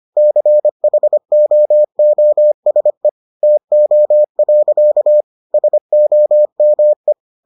Morse code, heard as a keyed tone, about 25 wpm, 600 Hz. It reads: CHOOSE TO. SOME